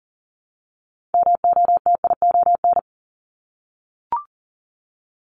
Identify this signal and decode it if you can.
Morse code, heard as a keyed tone, about 40 wpm, 700 Hz. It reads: MOTION
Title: Multi-letter Phonemes Repeated 3x 40wpm